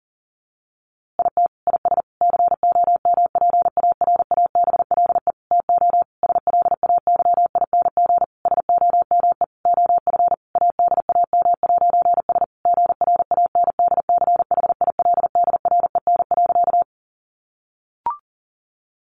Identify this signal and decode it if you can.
Morse code, heard as a keyed tone, about 40 wpm, 700 Hz. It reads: IT IS COMPARABLE TO SLAYING SOME OF ADAM'S GRANDCHILDREN.
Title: A Sound Of Thunder 40wpm